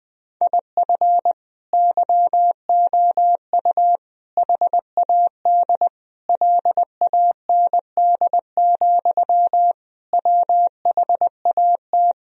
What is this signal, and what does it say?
Morse code, heard as a keyed tone, about 21 wpm, 700 Hz. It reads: IF YOU HAD LAND, WHAT
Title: Sentences from Top 1000 Words - Review Part 1 - 20wpm